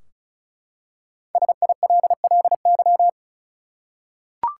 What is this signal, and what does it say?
Morse code, heard as a keyed tone, about 36 wpm, 700 Hz. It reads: SILLY
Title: Five Letter Words - 35wpm